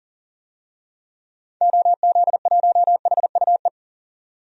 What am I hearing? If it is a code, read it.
Morse code, heard as a keyed tone, about 38 wpm, 700 Hz. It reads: OZ1HUE
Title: DXCC-100 Call Signs - 6 Characters - 40wpm